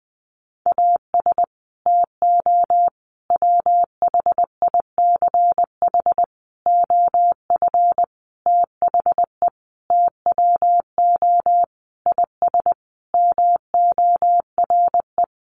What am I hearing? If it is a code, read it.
Morse code, heard as a keyed tone, about 20 wpm, 700 Hz. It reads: AS TO WHICH OF THE TWO IS MORE
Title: Sentences from Top 1200 Words 20wpm